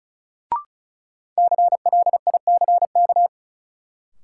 Morse code, heard as a keyed tone, about 34 wpm, 700 Hz. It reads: CLICK
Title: Sentences from Top 1100 Words - New Word Drill - 35wpm